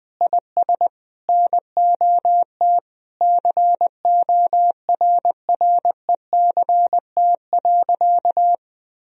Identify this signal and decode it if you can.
Morse code, heard as a keyed tone, about 20 wpm, 700 Hz. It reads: IS NOT CORRECT.
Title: Sentences from Top 1000 Words - Review Part 3 - 20wpm